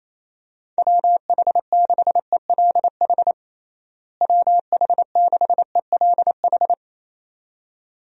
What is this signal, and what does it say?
Morse code, heard as a keyed tone, about 28 wpm, 700 Hz. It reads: WH6ELH WH6ELH
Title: USA Call Signs - 6 Characters - Repeated 2x - 28wpm